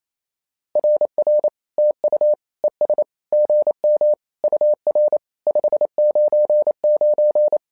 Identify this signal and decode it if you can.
Morse code, heard as a keyed tone, about 28 wpm, 600 Hz. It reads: RR TU ES GM UR 599
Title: SOTA QSO Practice 28wpm